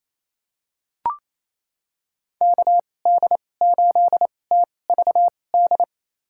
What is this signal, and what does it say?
Morse code, heard as a keyed tone, about 28 wpm, spaced as for 20 wpm, 700 Hz. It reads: KD8TVD
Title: USA Call Signs - 6 Characters - 2x Character Spacing - 28wpm